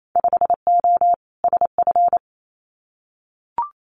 Morse code, heard as a keyed tone, about 28 wpm, 700 Hz. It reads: 5O SF
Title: Winter Field Day Exchanges Encore 28wpm